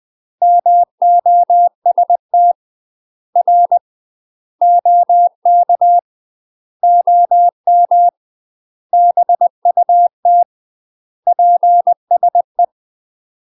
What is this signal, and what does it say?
Morse code, heard as a keyed tone, about 20 wpm, 700 Hz. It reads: MOST R OK OM BUT PSE
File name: UK MASTER QSOs TRAINING / 20WPM / UK MASTER QSOs 0041